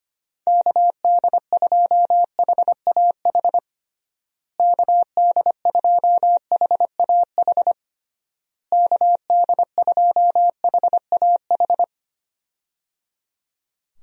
Morse code, heard as a keyed tone, about 25 wpm, 700 Hz. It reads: KD2HAH KD2HAH KD2HAH
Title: USA Call Signs - 6 Characters - Repeated 3x - 25wpm